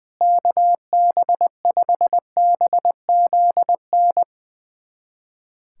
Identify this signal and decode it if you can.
Morse code, heard as a keyed tone, about 21 wpm, 700 Hz. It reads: KB5BZN
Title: USA Callsign DE Callsign - 6 Characters - 20wpm